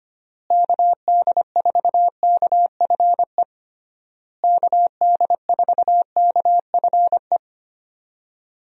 Morse code, heard as a keyed tone, about 25 wpm, 700 Hz. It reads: KD4KFE KD4KFE
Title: USA Call Signs - 6 Characters - Repeated 3x - 25wpm